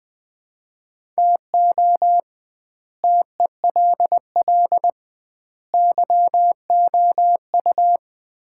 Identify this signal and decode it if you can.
Morse code, heard as a keyed tone, about 20 wpm, 700 Hz. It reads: TO TELL YOU
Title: 2x Word-Spacing - Sets of 3 Words - 20wpm